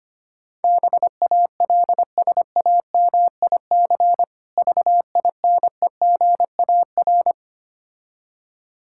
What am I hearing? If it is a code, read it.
Morse code, heard as a keyed tone, about 25 wpm, 700 Hz. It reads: BALSAMIC VINEGAR
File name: practice / 25wpm / Difficult Words with Hints - 25wpm